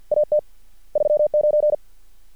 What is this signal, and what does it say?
Morse code, heard as a keyed tone, about 50 wpm, 600 Hz. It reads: AT 39
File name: archived / CRSnet